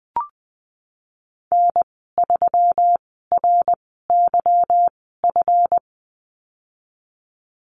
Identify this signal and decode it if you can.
Morse code, heard as a keyed tone, about 20 wpm, spaced as for 14 wpm, 700 Hz. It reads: N3RYF
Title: 161 Lesson 26g - Letter g - Call Signs 20wpm